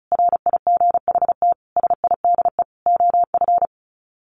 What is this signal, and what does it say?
Morse code, heard as a keyed tone, about 34 wpm, 700 Hz. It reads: RIGHT SIDE OF
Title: Sets of 5 Words Encore 35wpm